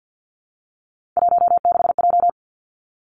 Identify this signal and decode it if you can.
Morse code, heard as a keyed tone, about 50 wpm, 700 Hz. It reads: 16J
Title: Three Challenging Letter-Number Combinations 50wpm